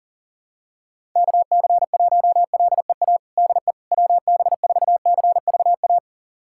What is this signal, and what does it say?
Morse code, heard as a keyed tone, about 40 wpm, 700 Hz. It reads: KC1LEA DE WB4CVA
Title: USA Callsign DE Callsign 40wpm